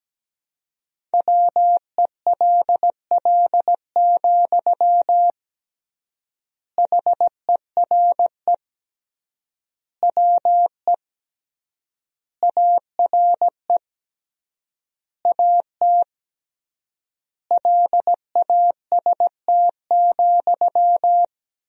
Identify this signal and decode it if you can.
Morse code, heard as a keyed tone, about 17 wpm, 700 Hz. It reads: WELL, HERE WE ARE AT LAST,
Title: 3x Word-Spacing - Sentences from Top 300 Words - 17wpm